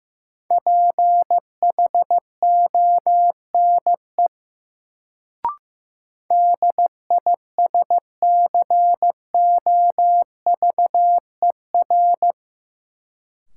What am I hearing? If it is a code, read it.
Morse code, heard as a keyed tone, about 15 wpm, 700 Hz. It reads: PHONE DISCOVER
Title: Sentences from Top 1100 Words - New Word Drill - 15wpm